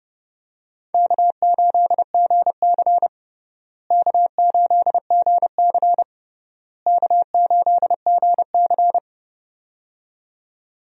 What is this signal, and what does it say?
Morse code, heard as a keyed tone, about 32 wpm, 700 Hz. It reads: K8GC K8GC K8GC
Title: USA Call Signs - 4 Characters - Repeated 3x - 30wpm